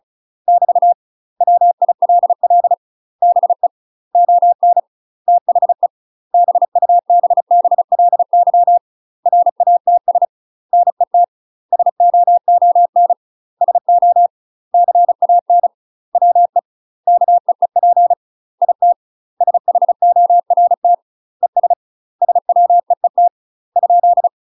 Morse code, heard as a keyed tone, about 37 wpm, 700 Hz. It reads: <BT> WILL BE ON THE BUBBLY RATS NET SOON SO CAN WE KEEP IT SHORT ES SWEET ?
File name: UK MASTER QSOs TRAINING / 35WPM / UK MASTER QSOs 0015